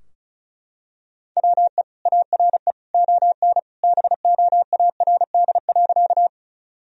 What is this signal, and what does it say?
Morse code, heard as a keyed tone, about 34 wpm, 700 Hz. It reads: WE ARE ON BOARD.
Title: Sentences from Top 1000 Words - Review Part 4 - 35wpm